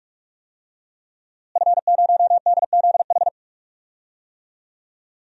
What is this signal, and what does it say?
Morse code, heard as a keyed tone, about 43 wpm, 700 Hz. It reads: F0DZH
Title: DXCC-100 Call Signs - 5 Characters - Repeated 2x - 45wpm